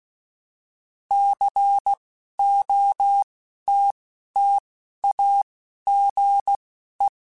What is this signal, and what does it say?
Morse code, heard as a keyed tone, about 16 wpm, spaced as for 11 wpm, 800 Hz. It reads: COTTAGE